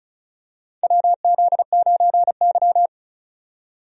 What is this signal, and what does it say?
Morse code, heard as a keyed tone, about 36 wpm, 700 Hz. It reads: WZ9Y
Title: USA Call Signs - 4 Characters - 35wpm